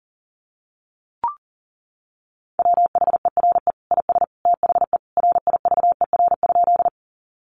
Morse code, heard as a keyed tone, about 40 wpm, 700 Hz. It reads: WHERE IS THE RIVER?
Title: Sentences from Top 300 Words 40wpm